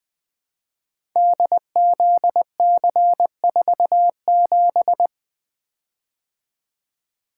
Morse code, heard as a keyed tone, about 20 wpm, 700 Hz. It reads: DZC47
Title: Code Groups - 5 Letter:Numbers - 20wpm